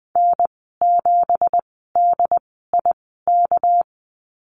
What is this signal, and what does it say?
Morse code, heard as a keyed tone, about 20 wpm, spaced as for 14 wpm, 700 Hz. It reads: N7DIK
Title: 213 Lesson 34g - Number 8 - Call Signs 20wpm